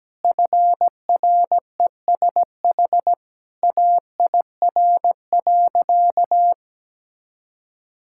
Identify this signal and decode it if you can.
Morse code, heard as a keyed tone, about 17 wpm, 700 Hz. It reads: FRESH AIR.
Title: Sentences from Top 1600 Words 17wpm